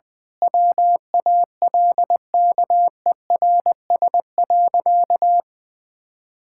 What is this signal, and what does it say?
Morse code, heard as a keyed tone, about 20 wpm, 700 Hz. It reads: WALKERS.
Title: Pangram Challenge v2 - 20wpm